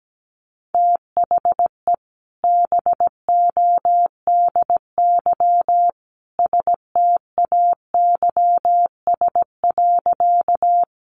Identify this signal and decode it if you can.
Morse code, heard as a keyed tone, about 17 wpm, 700 Hz. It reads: THE BODY STAYS.
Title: A Sound Of Thunder 17wpm